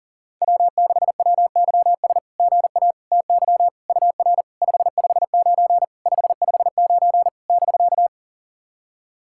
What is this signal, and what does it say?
Morse code, heard as a keyed tone, about 40 wpm, 690 Hz. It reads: W6WYS GA TY UR 559 559 <BK>
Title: SOTA QSO Practice 40wpm